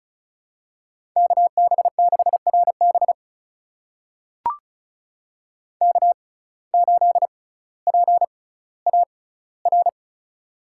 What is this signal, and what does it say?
Morse code, heard as a keyed tone, about 36 wpm, spaced as for 18 wpm, 700 Hz. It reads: KB6RB K8PAR
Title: USA Call Signs - 5 Characters - 6x Character Spacing - 35wpm